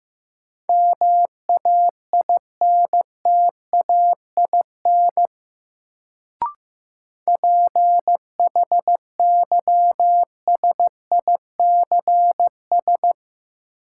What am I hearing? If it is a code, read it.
Morse code, heard as a keyed tone, about 15 wpm, 700 Hz. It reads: MAINTAIN PHYSICS
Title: Sentences from Top 1000 Words - Review Part 5 - New Word Drill - 15wpm